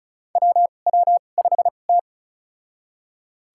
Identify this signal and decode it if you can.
Morse code, heard as a keyed tone, about 34 wpm, spaced as for 25 wpm, 700 Hz. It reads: WW5T
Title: USA Call Signs - 2x Character Spacing - 35wpm